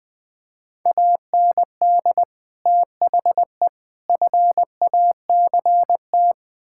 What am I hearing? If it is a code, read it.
Morse code, heard as a keyed tone, about 20 wpm, 700 Hz. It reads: AND THE FACT
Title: Build A Phrase - 4-Word Phrases - 20wpm